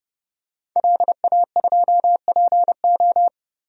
Morse code, heard as a keyed tone, about 30 wpm, 700 Hz. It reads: LA2PO
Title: DXCC-100 Call Signs - 5 Characters - Repeated 3x - 30wpm